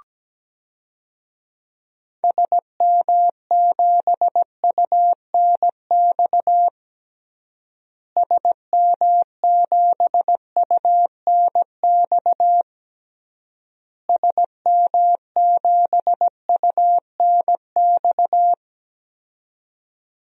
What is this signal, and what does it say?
Morse code, heard as a keyed tone, about 17 wpm, 700 Hz. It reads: SM7UNX SM7UNX SM7UNX
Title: DXCC-100 Call Signs - 6 Characters - Repeated 3x - 17wpm